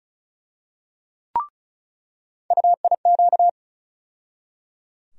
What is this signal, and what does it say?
Morse code, heard as a keyed tone, about 34 wpm, 700 Hz. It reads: UIQ
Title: USA Call Sign Postfixes - 3 Letters 35wpm